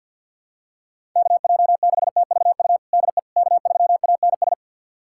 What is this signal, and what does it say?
Morse code, heard as a keyed tone, about 51 wpm, 700 Hz. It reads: KJ6TVU DE K3ANS
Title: USA Callsign DE Callsign 50wpm